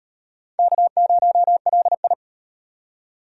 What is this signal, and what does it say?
Morse code, heard as a keyed tone, about 33 wpm, 700 Hz. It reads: K0LI